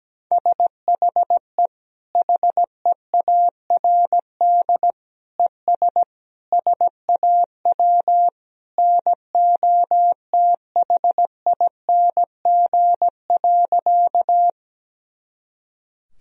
Morse code, heard as a keyed tone, about 17 wpm, 700 Hz. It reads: SHE HEARD ES SAW NOTHING.
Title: Sentences from Top 500 Words and Abbreviations 17wpm